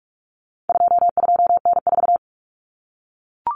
Morse code, heard as a keyed tone, about 44 wpm, 700 Hz. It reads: 22N4
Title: Common Cut Numbers - 4-Digit 45wpm